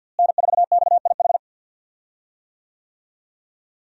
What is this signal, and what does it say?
Morse code, heard as a keyed tone, about 47 wpm, 700 Hz. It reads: N4XIH
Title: USA Call Signs - 5 Characters - Repeated 2x - 50wpm